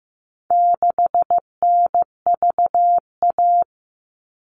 Morse code, heard as a keyed tone, about 15 wpm, 700 Hz. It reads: 6NVA
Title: All Cut Numbers - 4-Digit 15wpm